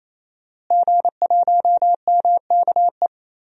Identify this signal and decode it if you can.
Morse code, heard as a keyed tone, about 28 wpm, 700 Hz. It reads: G1MKE
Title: DXCC-100 Call Signs - 5 Characters - Repeated 3x - 28wpm